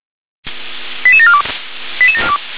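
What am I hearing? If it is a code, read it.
Five-tone selective call (ZVEI), heard as quick runs of five tones, tones of about 80 ms. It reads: 88742
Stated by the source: ZVEI five-tone selective call received on an SDR.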